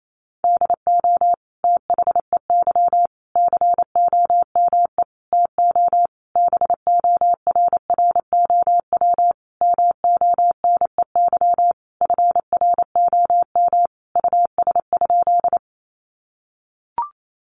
Morse code, heard as a keyed tone, about 28 wpm, 700 Hz. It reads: DO THEY COME TO BORROW MONEY FROM US?
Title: Sentences from Top 1500 Words 28wpm